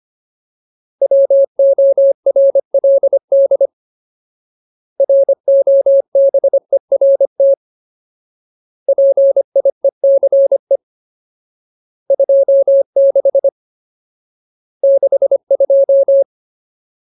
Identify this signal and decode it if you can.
Morse code, heard as a keyed tone, about 25 wpm, 550 Hz. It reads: WORLD ROBERT PIECE 26 62
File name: Ultimate Word List / 25WPM Ultimate